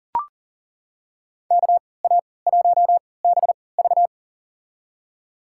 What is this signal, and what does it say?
Morse code, heard as a keyed tone, about 40 wpm, spaced as for 23 wpm, 700 Hz. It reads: KA1BV
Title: USA Call Signs - 5 Characters - 3x Character Spacing - 40wpm